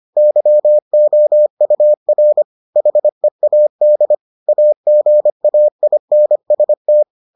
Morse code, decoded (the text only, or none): YOUR HEAD AGAINST